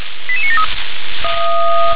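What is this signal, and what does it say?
Five-tone selective call (ZVEI), heard as quick runs of five tones, tones of about 70 ms. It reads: 88973
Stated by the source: ZVEI five-tone selective call received on SDR.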